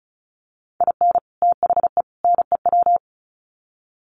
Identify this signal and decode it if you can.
Morse code, heard as a keyed tone, about 34 wpm, 700 Hz. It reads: IN THE NEW
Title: Sets of 3 Words 35wpm